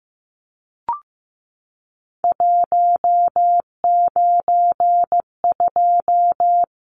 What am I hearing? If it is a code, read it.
Morse code, heard as a keyed tone, about 15 wpm, 700 Hz. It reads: 192
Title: Numbers - 3-Digit 15wpm